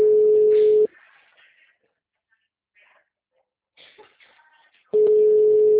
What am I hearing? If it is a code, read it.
Ringback tone (Europe, 425 Hz).
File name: out- -2011